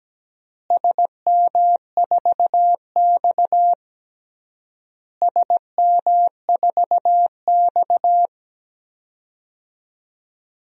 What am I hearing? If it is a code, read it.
Morse code, heard as a keyed tone, about 17 wpm, 700 Hz. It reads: SM4X SM4X
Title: DXCC-100 Call Signs - 4 Characters - Repeated 3x - 17wpm